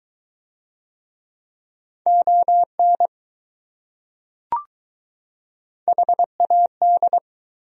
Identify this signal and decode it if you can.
Morse code, heard as a keyed tone, about 20 wpm, 700 Hz. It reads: ON HAD